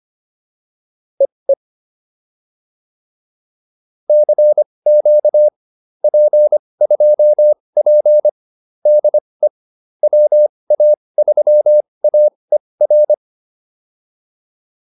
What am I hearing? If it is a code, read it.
Morse code, heard as a keyed tone, about 25 wpm, spaced as for 20 wpm, 600 Hz. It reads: EE CQ P2P DE WA3AER